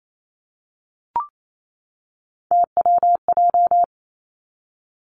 Morse code, heard as a keyed tone, about 28 wpm, 690 Hz. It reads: TWJ
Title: Three Challenging Letter-Number Combinations 28wpm